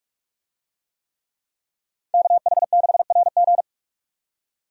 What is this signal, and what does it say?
Morse code, heard as a keyed tone, about 44 wpm, 700 Hz. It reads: KH6RG